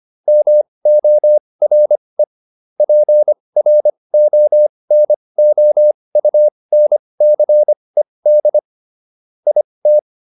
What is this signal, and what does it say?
Morse code, heard as a keyed tone, about 25 wpm, spaced as for 20 wpm, 600 Hz. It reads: MORE PRONOUNCED IT